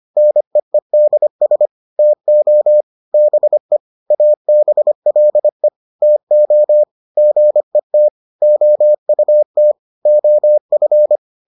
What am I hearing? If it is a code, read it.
Morse code, heard as a keyed tone, about 25 wpm, 600 Hz. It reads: NEEDS TO BE ABLE TO GET OUT OF